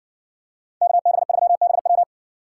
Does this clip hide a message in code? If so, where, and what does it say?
Yes: Morse code from 0.8 to 2.0 s, heard as a keyed tone, about 60 wpm, 700 Hz. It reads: KB3BW